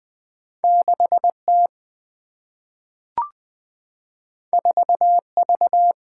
Morse code, heard as a keyed tone, about 20 wpm, 700 Hz. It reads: 6 T 4 V
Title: Single-Letter Single-Number 20wpm